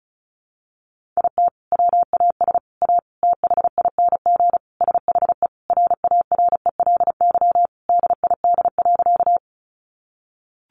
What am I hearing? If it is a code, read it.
Morse code, heard as a keyed tone, about 34 wpm, 700 Hz. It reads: IT WAS A THING SHE RARELY DID.
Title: Sentences from Top 1400 Words 35wpm